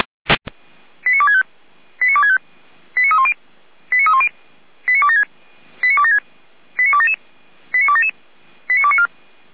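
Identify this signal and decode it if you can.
Five-tone selective call (ZVEI), heard as quick runs of five tones, tones of about 70 ms. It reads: 89265 89265 89310 89310 89276 89276 89277 89277 89294
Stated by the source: ZVEI five-tone selective call received with an SDR, archived by Feuerwehr Orlamünde.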